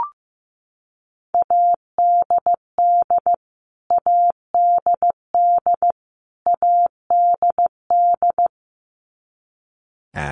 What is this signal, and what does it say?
Morse code, heard as a keyed tone, about 15 wpm, 700 Hz. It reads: ADD ADD ADD
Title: Top 300 Words Repeated 15wpm